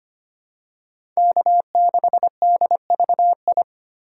Morse code, heard as a keyed tone, about 25 wpm, 700 Hz. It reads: K6DVI
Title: USA Call Signs - 5 Characters - 2x Character Spacing - 25wpm